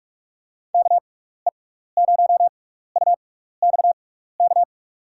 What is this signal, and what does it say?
Morse code, heard as a keyed tone, about 41 wpm, spaced as for 15 wpm, 700 Hz. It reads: KE0UXK
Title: USA Call Signs - 6x Character Spacing - 45wpm